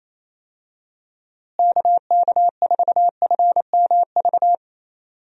Morse code, heard as a keyed tone, about 28 wpm, 700 Hz. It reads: KK4FMV
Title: USA Call Signs - 6 Characters - Repeated 2x - 28wpm